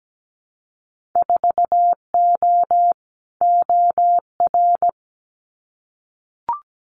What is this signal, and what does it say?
Morse code, heard as a keyed tone, about 17 wpm, 700 Hz. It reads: 4O OR
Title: Winter Field Day Exchanges Encore 17wpm